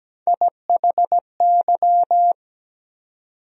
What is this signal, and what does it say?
Morse code, heard as a keyed tone, about 17 wpm, 700 Hz. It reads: IHY
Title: USA Call Sign Postfixes - 3 Letters 17wpm